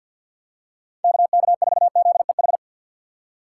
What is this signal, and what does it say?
Morse code, heard as a keyed tone, about 54 wpm, 690 Hz. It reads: KK4ZEH
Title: USA Call Signs - 6 Characters - 8x Character Spacing - 50wpm